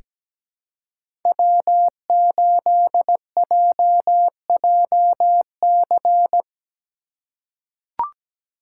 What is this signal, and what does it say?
Morse code, heard as a keyed tone, about 17 wpm, 700 Hz. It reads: W8JJC
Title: USA Call Signs - 5 Characters - 4x Character Spacing - 17wpm